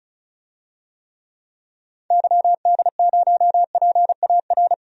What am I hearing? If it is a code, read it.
Morse code, heard as a keyed tone, about 34 wpm, 700 Hz. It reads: YD0PAR
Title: DXCC-100 Call Signs - 6 Characters - Repeated 2x - 35wpm